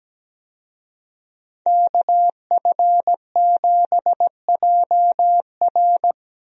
Morse code, heard as a keyed tone, about 17 wpm, 700 Hz. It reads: KF7JR
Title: USA Call Signs - 5 Characters - 17wpm